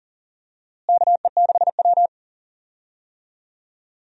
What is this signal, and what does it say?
Morse code, heard as a keyed tone, about 40 wpm, 700 Hz. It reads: KE6W
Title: USA Call Signs - 4 Characters - Repeated 2x - 40wpm